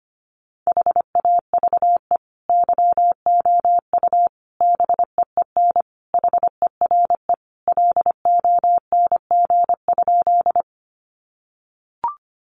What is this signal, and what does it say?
Morse code, heard as a keyed tone, about 25 wpm, 700 Hz. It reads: HAVE YOU BEEN HERE LONG?
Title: Sentences from Top 200 Words 25wpm